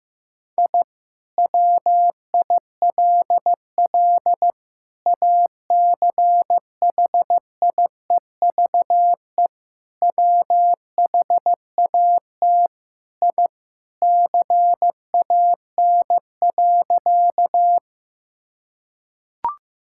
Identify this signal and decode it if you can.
Morse code, heard as a keyed tone, about 15 wpm, 700 Hz. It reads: I WILL ACHIEVE WHAT I CAN.
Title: Sentences from Top 1000 Words - Review Part 6 - 15wpm